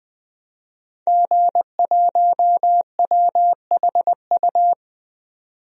Morse code, heard as a keyed tone, about 20 wpm, 700 Hz. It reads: G1WHU
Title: DXCC-100 Call Signs - 5 Characters - 20wpm